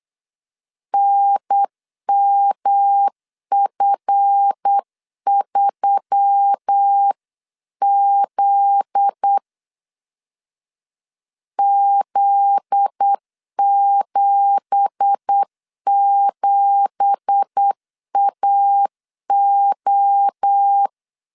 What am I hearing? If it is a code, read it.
Morse code, heard as a keyed tone, about 8 wpm, 800 Hz. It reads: NMF3Z Z77AO
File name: LEZIONI / 9 2867 8wpm 3440